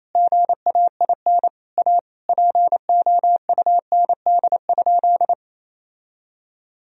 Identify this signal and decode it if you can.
Morse code, heard as a keyed tone, about 27 wpm, 700 Hz. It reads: GAIN A POUND?
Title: Sentences from Top 1000 Words - Review Part 2 - 28wpm